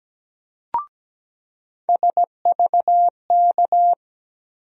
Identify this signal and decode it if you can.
Morse code, heard as a keyed tone, about 17 wpm, 700 Hz. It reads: SVK
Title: USA Call Sign Postfixes - 3 Letters 17wpm